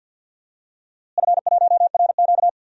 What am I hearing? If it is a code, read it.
Morse code, heard as a keyed tone, about 52 wpm, 700 Hz. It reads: F1RQ
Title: DXCC-100 Call Signs - 4 Characters - Repeated 2x - 50wpm